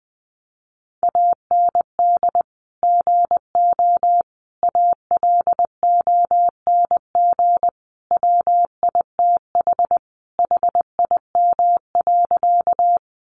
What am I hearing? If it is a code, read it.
Morse code, heard as a keyed tone, about 20 wpm, 700 Hz. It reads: AND GO ALONG WITH HIM.
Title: Sentences from Top 700 Words 20wpm